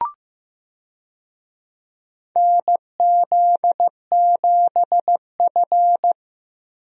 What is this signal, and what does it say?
Morse code, heard as a keyed tone, about 16 wpm, 700 Hz. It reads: NZ7F
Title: USA Call Signs - 4 Characters - 15wpm